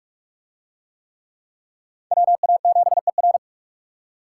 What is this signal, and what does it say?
Morse code, heard as a keyed tone, about 44 wpm, 700 Hz. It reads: WA7ER